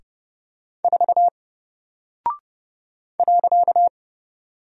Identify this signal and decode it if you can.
Morse code, heard as a keyed tone, about 30 wpm, 700 Hz. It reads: 4 .
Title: Single Character (Letter:Number:Punctuation:Prosign) Alt 30wpm